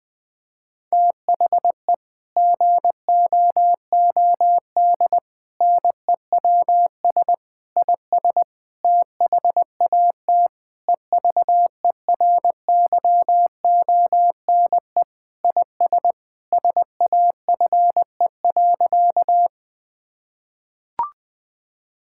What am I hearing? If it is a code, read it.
Morse code, heard as a keyed tone, about 20 wpm, 700 Hz. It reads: THE GOOD NEWS IS THAT EVERYONE IS SAFE.
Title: Sentences with 5-Word Phrases - 20wpm